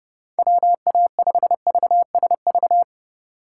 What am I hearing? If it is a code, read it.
Morse code, heard as a keyed tone, about 29 wpm, 700 Hz. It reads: WA5VSV